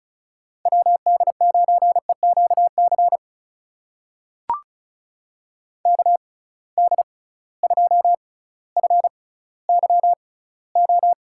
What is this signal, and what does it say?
Morse code, heard as a keyed tone, about 36 wpm, spaced as for 18 wpm, 700 Hz. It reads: WD9EQC KD2FYO
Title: USA Call Signs - 6 Characters - 6x Character Spacing - 35wpm